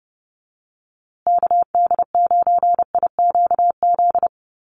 Morse code, heard as a keyed tone, about 30 wpm, 700 Hz. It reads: KD9IQZ
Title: USA Call Signs Repeated 2x 30wpm